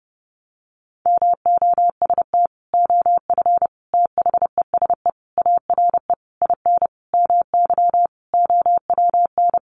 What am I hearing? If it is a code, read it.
Morse code, heard as a keyed tone, about 30 wpm, 700 Hz. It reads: MOST OF THESE ARE IN MY OWN
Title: Sentences from Top 1200 Words 30wpm